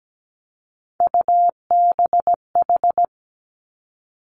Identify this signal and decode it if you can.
Morse code, heard as a keyed tone, about 17 wpm, 700 Hz. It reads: UBH
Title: Three Challenging Letter-Number Combinations 17wpm